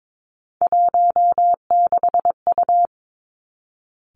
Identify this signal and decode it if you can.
Morse code, heard as a keyed tone, about 22 wpm, 700 Hz. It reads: 16U
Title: Three Challenging Letter-Number Combinations 22wpm